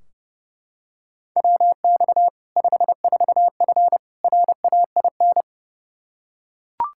Morse code, heard as a keyed tone, about 30 wpm, 700 Hz. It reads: WX 54F RAIN
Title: Practice QSO - Exchange 2 - Info - 30wpm